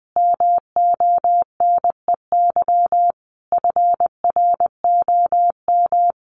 Morse code, heard as a keyed tone, about 20 wpm, 700 Hz. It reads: MONEY FROM
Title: Sentences from Top 1500 Words 20wpm